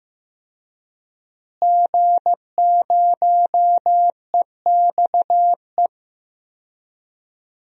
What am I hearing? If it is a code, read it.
Morse code, heard as a keyed tone, about 15 wpm, 700 Hz. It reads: G0EXE
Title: DXCC-100 Call Signs - 5 Characters - Repeated 2x - 15wpm